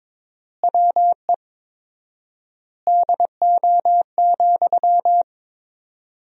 Morse code, heard as a keyed tone, about 22 wpm, 700 Hz. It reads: WE DO,
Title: 4x Word-Spacing - Sentences from Top 100 Words - 22wpm